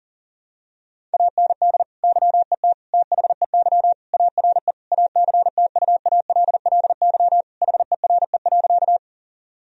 Morse code, heard as a keyed tone, about 40 wpm, 700 Hz. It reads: AND YET THEY ARE ACTUALLY HERE.
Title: Sentences from Top 1000 Words - Review Part 4 - 40wpm